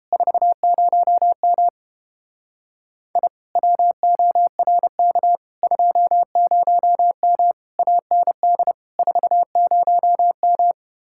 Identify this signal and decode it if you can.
Morse code, heard as a keyed tone, about 29 wpm, 700 Hz. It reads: 40M I WORK 20M AND 40M